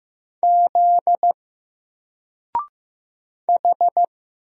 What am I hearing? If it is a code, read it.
Morse code, heard as a keyed tone, about 15 wpm, 700 Hz. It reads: Z H
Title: Single Letter-Number Alt 15wpm